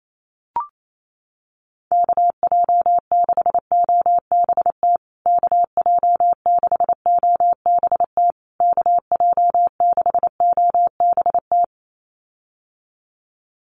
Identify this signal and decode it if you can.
Morse code, heard as a keyed tone, about 28 wpm, 700 Hz. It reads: KJ6OBT KJ6OBT KJ6OBT
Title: USA Call Signs Repeated 3x 28wpm